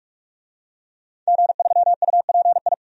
Morse code, heard as a keyed tone, about 46 wpm, 700 Hz. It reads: G3UPI
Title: DXCC-100 Call Signs - 5 Characters - 45wpm